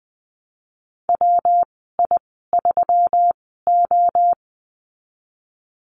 Morse code, heard as a keyed tone, about 19 wpm, spaced as for 14 wpm, 700 Hz. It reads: WI3O
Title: 161 Lesson 26g - Letter g - Call Signs 20wpm